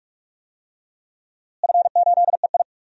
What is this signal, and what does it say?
Morse code, heard as a keyed tone, about 46 wpm, 700 Hz. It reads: F8EI
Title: DXCC-100 Call Signs - 4 Characters - Repeated 2x - 45wpm